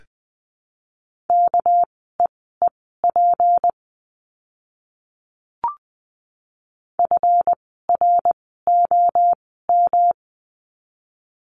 Morse code, heard as a keyed tone, about 20 wpm, spaced as for 14 wpm, 700 Hz. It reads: KEEP FROM
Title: 230 Lesson 37f - Letter x - Words 20wpm